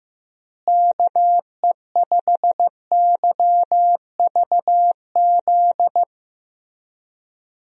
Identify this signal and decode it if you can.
Morse code, heard as a keyed tone, about 14 wpm, 700 Hz. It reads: KE5YVZ
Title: USA Call Signs - 6 Characters - Repeated 3x - 15wpm